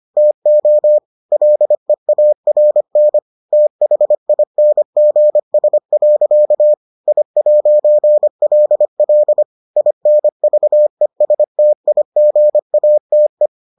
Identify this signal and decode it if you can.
Morse code, heard as a keyed tone, about 25 wpm, 600 Hz. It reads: TO LEARN THINGS. I'LL INVESTIGATE